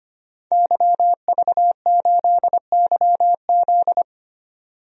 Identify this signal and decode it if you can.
Morse code, heard as a keyed tone, about 25 wpm, 700 Hz. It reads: YV8YZ
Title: DXCC-100 Call Signs - 5 Characters - 25wpm